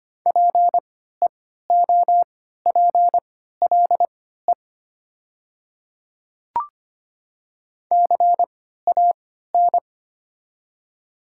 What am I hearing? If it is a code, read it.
Morse code, heard as a keyed tone, about 25 wpm, spaced as for 14 wpm, 700 Hz. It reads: PEOPLE CAN
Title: 3x Character Spacing - Top 100 Words - 25wpm